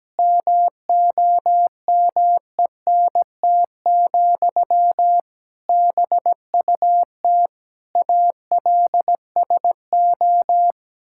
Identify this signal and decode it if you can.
Morse code, heard as a keyed tone, about 17 wpm, 700 Hz. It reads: MOMENT, BUT ALSO
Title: Sentences from Top 1000 Words - Review Part 6 - 17wpm